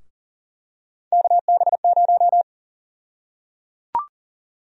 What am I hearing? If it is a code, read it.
Morse code, heard as a keyed tone, about 37 wpm, 700 Hz. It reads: KB0
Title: USA Call Sign Prefixes - Repeated 2x - 40wpm